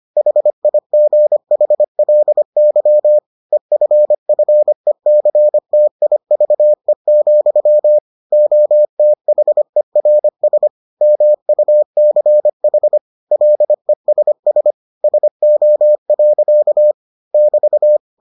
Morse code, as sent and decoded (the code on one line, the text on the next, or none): .... .. --. .... .-.. -.-- / . ..-. ..-. -.-. - .. ...- . --..-- / --- - .... .-. ... / -- ..- -.-. .... / .-.. ... ... / ... --- .-.-.- / -...-
HIGHLY EFFECTIVE, OTHERS MUCH LESS SO. <BT>